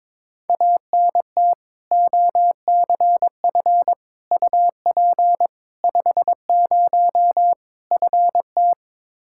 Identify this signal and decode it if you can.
Morse code, heard as a keyed tone, about 22 wpm, 700 Hz. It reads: ANT OCF UP 50 FT
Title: Practice QSO - Exchange 2 - Info - 22wpm